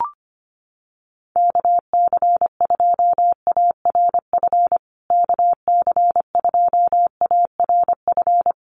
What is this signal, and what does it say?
Morse code, heard as a keyed tone, about 25 wpm, 700 Hz. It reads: KC2ARF KC2ARF
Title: USA Call Signs Repeated 2x 25wpm